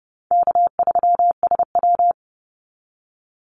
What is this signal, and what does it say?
Morse code, heard as a keyed tone, about 28 wpm, 700 Hz. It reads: K3SW
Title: Call Signs Repeated 3x - 30wpm